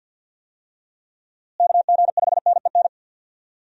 Morse code, heard as a keyed tone, about 48 wpm, 700 Hz. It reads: KG5NEN
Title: USA Call Signs - 6 Characters - Repeated 2x - 50wpm